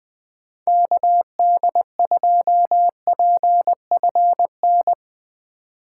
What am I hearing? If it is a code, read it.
Morse code, heard as a keyed tone, about 21 wpm, 700 Hz. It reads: KD2PFN
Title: USA Call Signs - 6 Characters - 2x Character Spacing - 20wpm